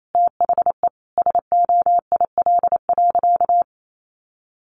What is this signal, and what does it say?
Morse code, heard as a keyed tone, about 28 wpm, 700 Hz. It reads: THE SOIL.
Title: Sentences from Top 1500 Words 28wpm